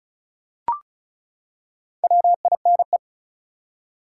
Morse code, heard as a keyed tone, about 34 wpm, 700 Hz. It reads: WINE
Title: Four Letter Words - 35wpm